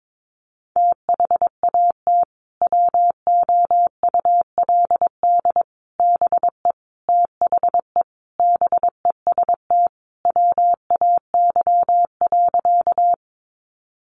Morse code, heard as a keyed tone, about 22 wpm, 700 Hz. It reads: THAT WOULD BE THE BEST WAY.
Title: Sentences from Top 400 Words 22wpm